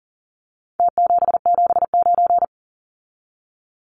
Morse code, heard as a keyed tone, about 40 wpm, 700 Hz. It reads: T779
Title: Numbers - Contest Style - 4-Digit 40wpm